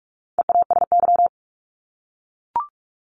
Morse code, heard as a keyed tone, about 44 wpm, 700 Hz. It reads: EASY
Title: LY Suffix - Top 100 Words with Base Words 45wpm